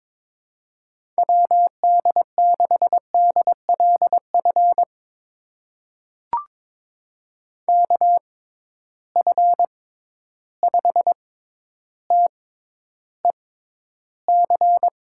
Morse code, heard as a keyed tone, about 22 wpm, spaced as for 11 wpm, 700 Hz. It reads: WD6DLF KF5TEC
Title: USA Call Signs - 6 Characters - 6x Character Spacing - 22wpm